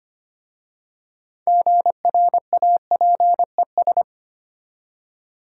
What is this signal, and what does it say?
Morse code, heard as a keyed tone, about 25 wpm, 690 Hz. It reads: GRAPES
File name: practice / 25wpm / Easy Words with Hints - 25wpm